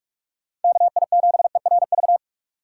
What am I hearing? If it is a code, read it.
Morse code, heard as a keyed tone, about 45 wpm, 700 Hz. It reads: KI7ERV